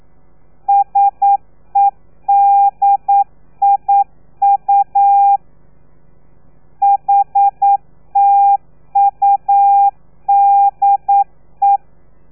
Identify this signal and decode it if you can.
Morse code, heard as a keyed tone, about 9 wpm, 800 Hz. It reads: SEDIU HTUDE
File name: telegrafia / i2tfj / Lez3 40